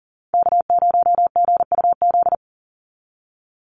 Morse code, heard as a keyed tone, about 40 wpm, 700 Hz. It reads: K0GUZ
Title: Call Signs Repeated 2x - 40wpm